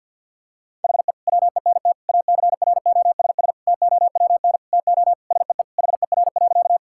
Morse code, heard as a keyed tone, about 50 wpm, 700 Hz. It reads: HE WENT ACROSS TOWN TO SEE HER.
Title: Sentences from Top 1000 Words - Review Part 5 - 50wpm